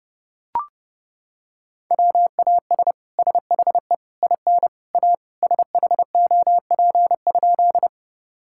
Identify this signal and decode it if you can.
Morse code, heard as a keyed tone, about 32 wpm, 700 Hz. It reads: WAS SHE IN A SHOP?
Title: Sentences from Top 1600 Words 30wpm